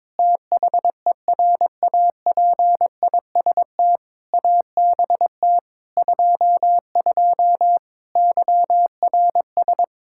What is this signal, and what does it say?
Morse code, heard as a keyed tone, about 21 wpm, 700 Hz. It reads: THERAPIST ABT 22 YRS
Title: Practice QSO - Exchange 3 - More Info - 22wpm